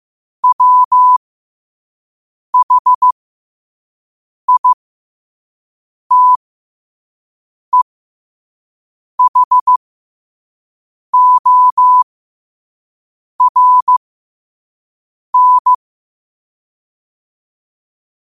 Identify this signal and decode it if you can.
Morse code, heard as a keyed tone, about 15 wpm, spaced as for 5 wpm, 1000 Hz. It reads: WHITEHORN
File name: qotdfiles / 5wpm